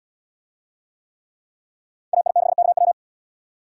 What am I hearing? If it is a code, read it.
Morse code, heard as a keyed tone, about 73 wpm, 700 Hz. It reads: AI6KW